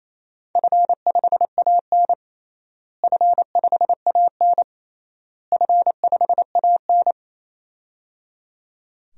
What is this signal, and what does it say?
Morse code, heard as a keyed tone, about 28 wpm, 700 Hz. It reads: F5AN F5AN F5AN
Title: DXCC-100 Call Signs - 4 Characters - Repeated 3x - 28wpm